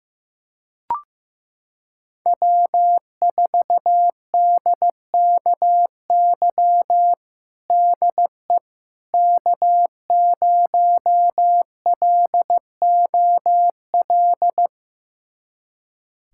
Morse code, heard as a keyed tone, about 15 wpm, 700 Hz. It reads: W4DKY DE K0LOL
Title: USA Callsign DE Callsign - 5 Characters - 15wpm